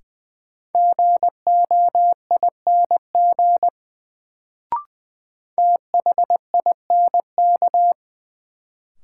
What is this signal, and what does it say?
Morse code, heard as a keyed tone, about 21 wpm, 700 Hz. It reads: GOING THINK
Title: Sets of 2 Words - New Word Drill - 20wpm